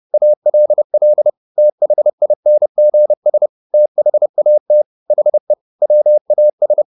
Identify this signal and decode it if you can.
Morse code, heard as a keyed tone, about 30 wpm, 600 Hz. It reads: ALL THINGS THAT HE WAS